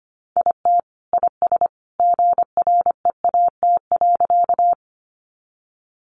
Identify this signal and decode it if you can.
Morse code, heard as a keyed tone, about 25 wpm, 700 Hz. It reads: IT IS GREAT.
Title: Sentences from Top 200 Words 25wpm